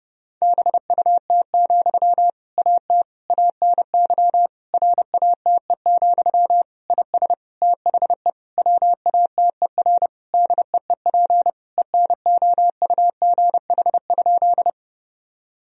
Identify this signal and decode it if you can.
Morse code, heard as a keyed tone, about 30 wpm, 700 Hz. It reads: BUT, AT ANY RATE, IS THE WATER DEEP ENOUGH?
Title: Sentences from Top 1000 Words - Review Part 4 - 30wpm